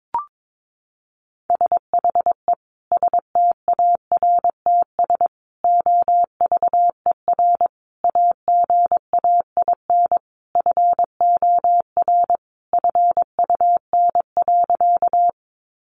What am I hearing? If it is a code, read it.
Morse code, heard as a keyed tone, about 22 wpm, 700 Hz. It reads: SHE STARTS OVER AGAIN FOR FUN.
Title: Easy Phrases 22wpm